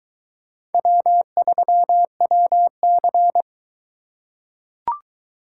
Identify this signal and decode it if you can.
Morse code, heard as a keyed tone, about 23 wpm, 700 Hz. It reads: W3WC